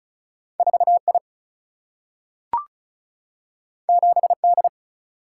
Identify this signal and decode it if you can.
Morse code, heard as a keyed tone, about 36 wpm, 700 Hz. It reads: 4I 7D
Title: Single-Letter Single-Number 35wpm